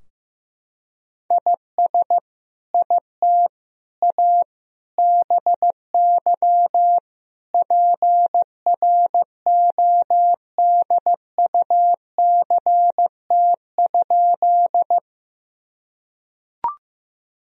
Morse code, heard as a keyed tone, about 15 wpm, 700 Hz. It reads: IS IT A BY PRODUCT?
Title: Sentences from Top 1000 Words - Review Part 2 - 15wpm